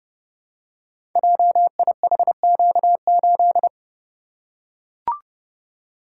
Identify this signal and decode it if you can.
Morse code, heard as a keyed tone, about 30 wpm, 700 Hz. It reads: JIHQ8
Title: Code Groups - 5 Letter:Numbers - 30wpm